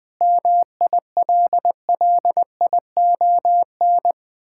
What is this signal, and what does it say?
Morse code, heard as a keyed tone, about 20 wpm, 700 Hz. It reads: MILLION
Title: Sentences with 9-Word Phrases - Spoken Beginning - 20wpm